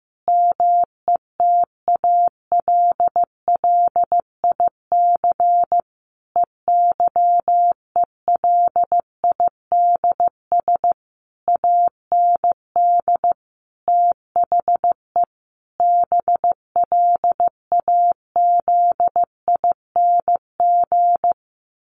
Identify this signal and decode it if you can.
Morse code, heard as a keyed tone, about 15 wpm, 700 Hz. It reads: METALLIC EYELIDS AND THE BLAZING
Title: A Sound Of Thunder 15wpm